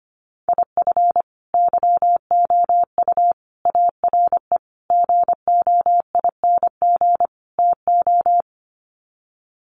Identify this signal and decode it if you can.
Morse code, heard as a keyed tone, about 25 wpm, 700 Hz. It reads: IF YOU ARE GOING TO
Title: Sets of 5 Words Encore2 25wpm